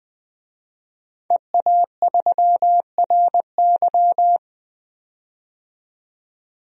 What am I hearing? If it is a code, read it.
Morse code, heard as a keyed tone, about 20 wpm, 700 Hz. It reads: EA3RY
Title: DXCC-100 Call Signs - 5 Characters - 20wpm